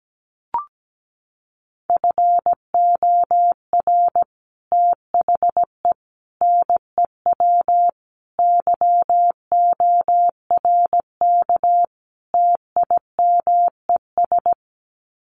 Morse code, heard as a keyed tone, about 17 wpm, 700 Hz. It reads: FOR THE NEW YORK TIMES
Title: Sets of 5 Words 17wpm